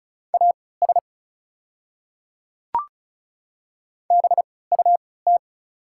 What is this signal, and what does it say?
Morse code, heard as a keyed tone, about 36 wpm, spaced as for 20 wpm, 690 Hz. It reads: AS BUT
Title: 3x Character Spacing - Top 100 Words - 35wpm